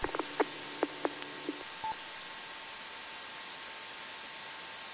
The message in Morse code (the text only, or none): none